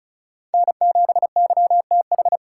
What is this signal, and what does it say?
Morse code, heard as a keyed tone, about 42 wpm, 700 Hz. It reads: N7YTH
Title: Speed Racing - USA 1x3 Call Signs 35wpm